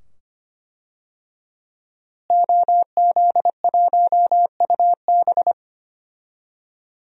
Morse code, heard as a keyed tone, about 25 wpm, 700 Hz. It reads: OZ1UB
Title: DXCC-100 Call Signs - 5 Characters - Repeated 2x - 25wpm